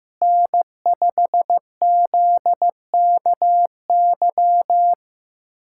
Morse code, heard as a keyed tone, about 15 wpm, 700 Hz. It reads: N5ZKY
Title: USA Callsign DE Callsign 15wpm